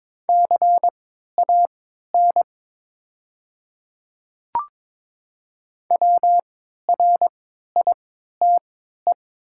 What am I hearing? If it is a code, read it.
Morse code, heard as a keyed tone, about 22 wpm, spaced as for 13 wpm, 690 Hz. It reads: CAN WRITE
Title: 3x Character Spacing - Top 100 Words - 22wpm